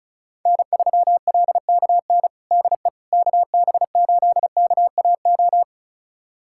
Morse code, heard as a keyed tone, about 34 wpm, 700 Hz. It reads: N3LKN DE KB8KAO
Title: USA Callsign DE Callsign 35wpm